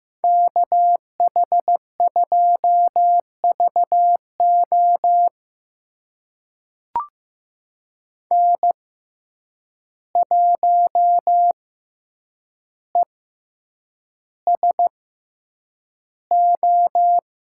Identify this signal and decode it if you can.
Morse code, heard as a keyed tone, about 15 wpm, spaced as for 8 wpm, 700 Hz. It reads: KH2VO N1ESO
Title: USA Call Signs - 5 Characters - 6x Character Spacing - 15wpm